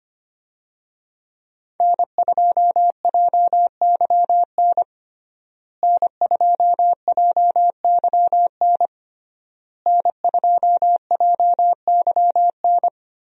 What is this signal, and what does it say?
Morse code, heard as a keyed tone, about 25 wpm, 700 Hz. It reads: N2JYN N2JYN N2JYN
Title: USA Call Signs - 5 Characters - Repeated 3x - 25wpm